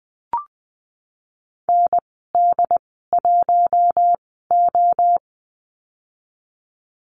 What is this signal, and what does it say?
Morse code, heard as a keyed tone, about 20 wpm, spaced as for 15 wpm, 700 Hz. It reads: ND1O
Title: 161 Lesson 26g - Letter g - Call Signs 20wpm